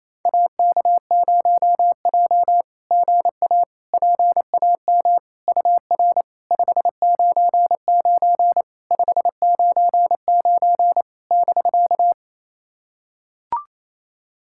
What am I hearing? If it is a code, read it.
Morse code, heard as a keyed tone, about 28 wpm, 700 Hz. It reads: AK0J GA PAM UR 599 599 <BK>
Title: SOTA QSO Practice 28wpm